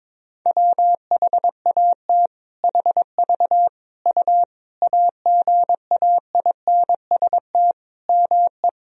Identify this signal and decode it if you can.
Morse code, heard as a keyed tone, about 21 wpm, 700 Hz. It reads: WHAT HV U AGAINST ME
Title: Sentences from Top 400 Words and Abbreviations 22wpm